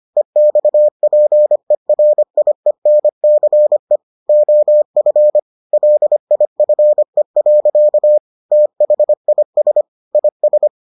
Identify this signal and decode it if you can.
Morse code, heard as a keyed tone, about 25 wpm, 600 Hz. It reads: EXPERIENCE OF LIFE. THIS IS